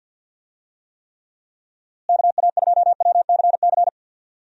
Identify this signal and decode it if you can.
Morse code, heard as a keyed tone, about 51 wpm, 700 Hz. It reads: KA2WCC
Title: USA Call Signs - 6 Characters - Repeated 3x - 50wpm